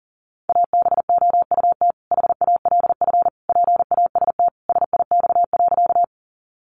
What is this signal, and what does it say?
Morse code, heard as a keyed tone, about 40 wpm, 700 Hz. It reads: ABOUT HALF PAST SIX.
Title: Sentences from Top 600 Words 40wpm